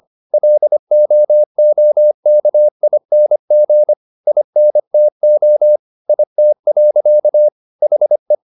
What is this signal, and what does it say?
Morse code, heard as a keyed tone, about 25 wpm, 600 Hz. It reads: LOOKING INTO IT. HE
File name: the old man and the sea part 1 25wpm